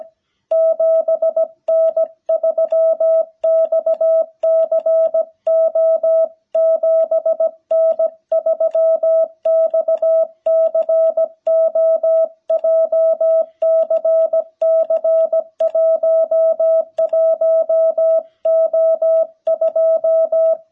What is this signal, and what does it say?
Morse code, heard as a keyed tone, about 17 wpm, 660 Hz. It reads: 7N3XCO 7N3XCO JCC11O2
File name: jh8cbh / 2021cw